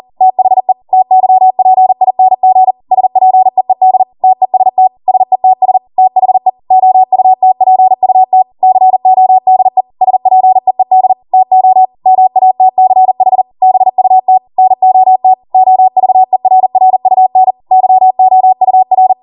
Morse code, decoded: THE TYPING SPEED TEST SETS THE OUTPUT CODE SPEED TO MATCH BUT NOT OVERRUN YOUR